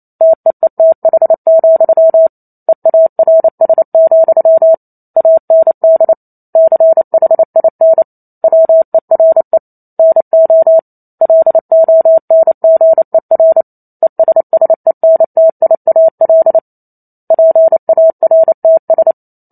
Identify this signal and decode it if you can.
Morse code, heard as a keyed tone, about 28 wpm, 650 Hz. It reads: TEETH, EARS, AND CHIN WERE NO LONGER ESSENTIAL PARTS
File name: Chapter 2 - 28 WPM